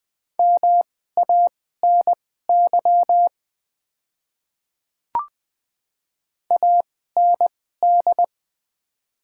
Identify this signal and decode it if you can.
Morse code, heard as a keyed tone, about 20 wpm, spaced as for 14 wpm, 690 Hz. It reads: MANY AND
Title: 2x Character Spacing - Top 200 Words - 20wpm